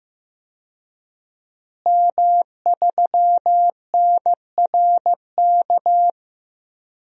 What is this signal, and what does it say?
Morse code, heard as a keyed tone, about 16 wpm, 700 Hz. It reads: M3NRK
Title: DXCC-100 Call Signs - 5 Characters - Repeated 3x - 15wpm